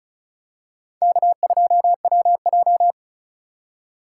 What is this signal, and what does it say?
Morse code, heard as a keyed tone, about 36 wpm, 700 Hz. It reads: K2WJ
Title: USA Call Signs - 4 Characters - 2x Character Spacing - 35wpm